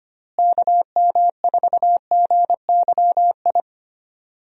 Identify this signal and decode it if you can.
Morse code, heard as a keyed tone, about 25 wpm, 700 Hz. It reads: KM4GYI
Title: USA Call Signs - 6 Characters - Repeated 2x - 25wpm